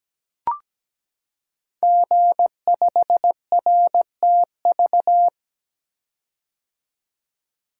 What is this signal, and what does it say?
Morse code, heard as a keyed tone, about 17 wpm, 700 Hz. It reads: G5RTV
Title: Code Groups - 5 Letter:Numbers - 17wpm